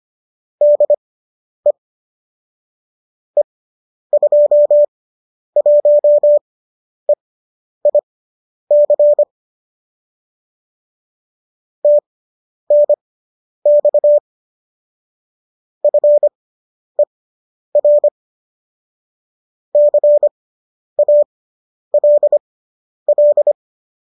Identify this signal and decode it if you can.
Morse code, heard as a keyed tone, about 25 wpm, spaced as for 10 wpm, 590 Hz. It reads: DE E21EIC TNX FER CALL
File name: qso005 10